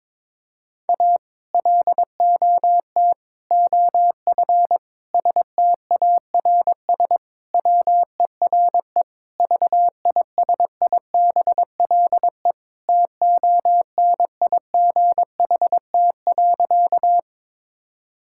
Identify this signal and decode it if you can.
Morse code, heard as a keyed tone, about 21 wpm, 700 Hz. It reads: A LOT OF STARS WERE VISIBLE TONIGHT.
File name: practice / 22wpm / Sentences with 2-Word Phrases - 22wpm